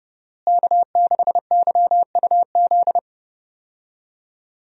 Morse code, heard as a keyed tone, about 30 wpm, 700 Hz. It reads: K6YUZ
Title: USA Call Signs - 5 Characters - Repeated 2x - 30wpm